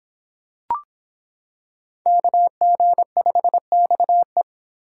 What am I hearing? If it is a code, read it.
Morse code, heard as a keyed tone, about 25 wpm, 700 Hz. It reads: KG5XE